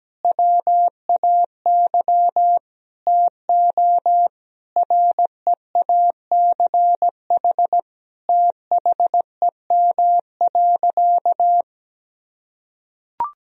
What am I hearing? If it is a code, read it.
Morse code, heard as a keyed tone, about 17 wpm, 700 Hz. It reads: WAY TO REACH THEM.
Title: Sentences from Top 1000 Words - Review Part 2 - 17wpm